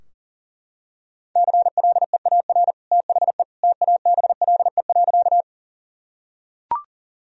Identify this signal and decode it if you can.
Morse code, heard as a keyed tone, about 40 wpm, 700 Hz. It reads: CLEAR THE TABLE.
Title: Sentences from Top 1000 Words - Review Part 2 - 40wpm